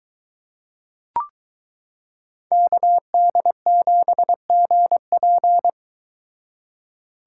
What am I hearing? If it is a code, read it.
Morse code, heard as a keyed tone, about 22 wpm, 700 Hz. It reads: KD7GP